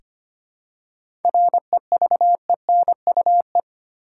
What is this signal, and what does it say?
Morse code, heard as a keyed tone, about 25 wpm, 700 Hz. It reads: REVENUE
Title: Seven Letter Words - 25wpm